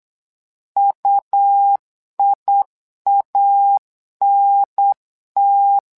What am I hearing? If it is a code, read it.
Morse code, heard as a keyed tone, about 8 wpm, 800 Hz. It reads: UIANT
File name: LEZIONI / 2 UHDN / 2 UHDN 8wpm2232